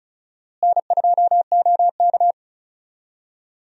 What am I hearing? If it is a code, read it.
Morse code, heard as a keyed tone, about 34 wpm, 700 Hz. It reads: N2OK
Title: USA Call Signs - 4 Characters - Repeated 3x - 35wpm